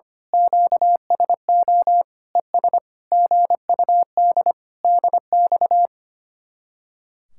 Morse code, heard as a keyed tone, about 24 wpm, 700 Hz. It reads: QSO ES GUD DX
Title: Speed-Racing Common QSO Phrases Encore - 25wpm